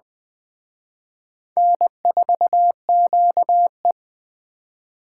Morse code, heard as a keyed tone, about 20 wpm, 700 Hz. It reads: N4QE